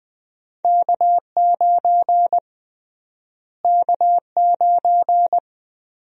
Morse code, heard as a keyed tone, about 20 wpm, 700 Hz. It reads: K9 K9
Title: USA Call Sign Prefixes - Repeated 2x - 20wpm